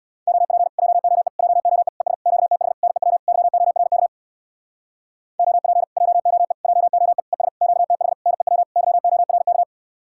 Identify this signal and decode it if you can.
Morse code, heard as a keyed tone, about 62 wpm, 700 Hz. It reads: OP JOE JOE ES QTH NEW YORK OP JOE JOE ES QTH NEW YORK
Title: Speed-Racing Common QSO Phrases - 50wpm